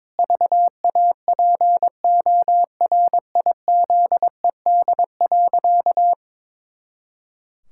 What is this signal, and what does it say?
Morse code, heard as a keyed tone, about 22 wpm, 700 Hz. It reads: VAPORIZED.
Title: Pangram Challenge v2 - 22wpm